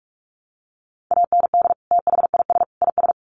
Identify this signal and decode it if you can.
Morse code, heard as a keyed tone, about 44 wpm, 700 Hz. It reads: AND THIS IS
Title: Sets of 3 Words 45wpm